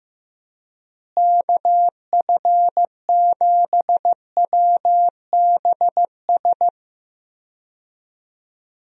Morse code, heard as a keyed tone, about 16 wpm, 700 Hz. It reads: KF7WBS
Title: USA Call Signs - 6 Characters - Repeated 3x - 15wpm